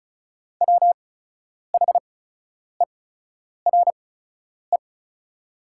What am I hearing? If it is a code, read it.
Morse code, heard as a keyed tone, about 36 wpm, spaced as for 10 wpm, 700 Hz. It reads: WHERE
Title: 8x Character Spacing - Top 200 Words - 35wpm